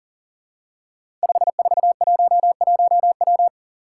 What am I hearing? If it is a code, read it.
Morse code, heard as a keyed tone, about 39 wpm, 700 Hz. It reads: 5411W
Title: Five Challenging Letter-Number Combinations 40wpm